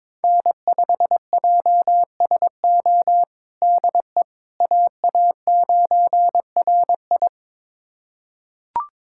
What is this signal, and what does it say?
Morse code, heard as a keyed tone, about 22 wpm, 700 Hz. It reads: N5JSO DE AA9RI
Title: USA Callsign DE Callsign - 5 Characters - 22wpm